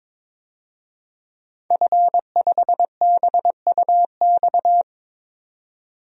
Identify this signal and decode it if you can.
Morse code, heard as a keyed tone, about 22 wpm, 700 Hz. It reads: F5BUX
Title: DXCC-100 Call Signs - 5 Characters - 22wpm